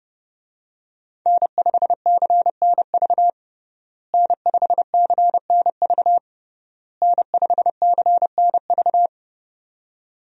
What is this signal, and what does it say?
Morse code, heard as a keyed tone, about 32 wpm, 700 Hz. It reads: N5CNV N5CNV N5CNV
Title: USA Call Signs - 5 Characters - Repeated 3x - 30wpm